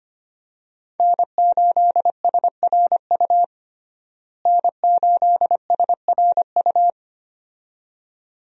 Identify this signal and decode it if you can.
Morse code, heard as a keyed tone, about 25 wpm, 700 Hz. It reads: N8SRU N8SRU
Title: USA Call Signs - 5 Characters - Repeated 2x - 25wpm